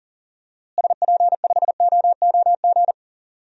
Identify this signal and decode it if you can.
Morse code, heard as a keyed tone, about 40 wpm, 700 Hz. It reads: SP5OOG
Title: DXCC-100 Call Signs - 6 Characters - Repeated 3x - 40wpm